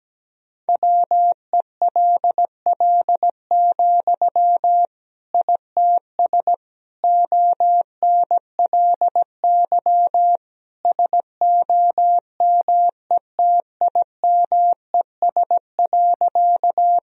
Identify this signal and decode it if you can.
Morse code, heard as a keyed tone, about 17 wpm, 700 Hz. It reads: WELL, ITS ONLY SOMETIMES.
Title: Sentences from Top 1000 Words - Review Part 3 - 17wpm